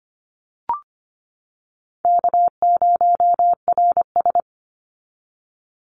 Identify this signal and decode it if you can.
Morse code, heard as a keyed tone, about 25 wpm, 700 Hz. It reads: K0RS
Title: NA SOTA Chaser Callsigns 25wpm